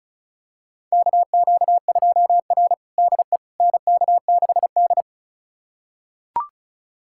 Morse code, heard as a keyed tone, about 34 wpm, 700 Hz. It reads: KQ2R DE NK6D
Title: USA Callsign DE Callsign - 4 Characters - 35wpm